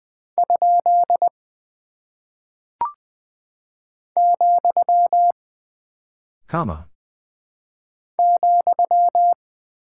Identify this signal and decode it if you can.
Morse code, heard as a keyed tone, about 20 wpm, 700 Hz. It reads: ? , ,
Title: Single Punctuation-Prosign - 20wpm